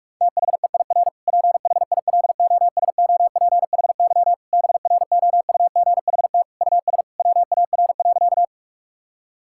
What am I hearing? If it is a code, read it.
Morse code, heard as a keyed tone, about 46 wpm, 700 Hz. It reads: THEIR PHILOSOPHY BROUGHT US WAR.
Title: Sentences from Top 1000 Words - Review Part 6 - 45wpm